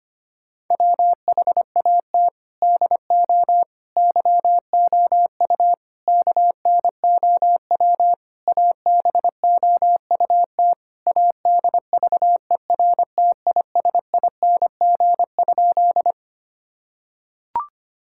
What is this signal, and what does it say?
Morse code, heard as a keyed tone, about 25 wpm, 700 Hz. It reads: WHAT DO YOU KNOW ABOUT ADVERTISING?
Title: Sentences from Top 1000 Words - Review Part 6 - 25wpm